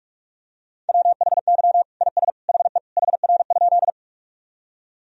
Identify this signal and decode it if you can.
Morse code, heard as a keyed tone, about 42 wpm, 700 Hz. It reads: WHY IS HE HR?
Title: Sentences from Top 200 Words and Abbreviations 45wpm